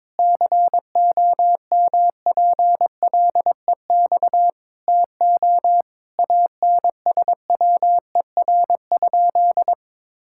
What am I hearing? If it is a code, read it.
Morse code, heard as a keyed tone, about 22 wpm, 700 Hz. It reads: COMPLEX TO ANSWER?
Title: Sentences from Top 1000 Words - Review Part 6 - 22wpm